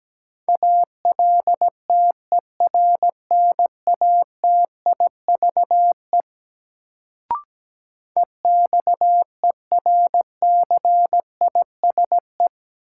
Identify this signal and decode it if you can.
Morse code, heard as a keyed tone, about 17 wpm, 700 Hz. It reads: ALTERNATIVE EXERCISE
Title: Sentences from Top 1000 Words - Review Part 4 - New Word Drill - 17wpm